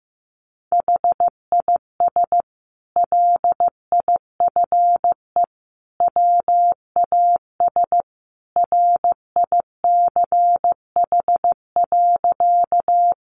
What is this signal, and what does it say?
Morse code, heard as a keyed tone, about 15 wpm, 700 Hz. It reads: HIS LIFE WAS RICH.
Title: Sentences from Top 1100 Words 15wpm